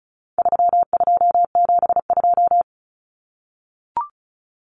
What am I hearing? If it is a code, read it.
Morse code, heard as a keyed tone, about 34 wpm, 700 Hz. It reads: 3272
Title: Common Cut Numbers - 4-Digit 35wpm